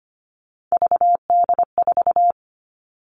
Morse code, heard as a keyed tone, about 25 wpm, 700 Hz. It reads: VD4
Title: All Cut Numbers - 3-Digit 25wpm